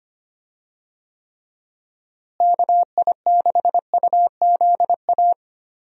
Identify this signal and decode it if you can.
Morse code, heard as a keyed tone, about 25 wpm, 700 Hz. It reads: KI6UZA
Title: USA Call Signs - 6 Characters - Repeated 2x - 25wpm